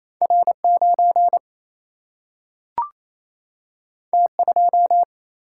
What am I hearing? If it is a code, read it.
Morse code, heard as a keyed tone, about 28 wpm, 700 Hz. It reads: R 9 T 2
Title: Single-Letter Single-Number 28wpm